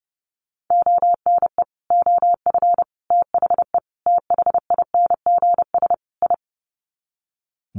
Morse code, heard as a keyed tone, about 29 wpm, 700 Hz. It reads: ONE OF THE THINGS I
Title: Sets of 5 Words 30wpm